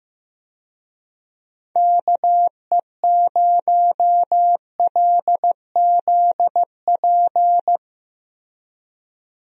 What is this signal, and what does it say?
Morse code, heard as a keyed tone, about 15 wpm, 700 Hz. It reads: KE0LZP
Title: USA Call Signs - 6 Characters - Repeated 3x - 15wpm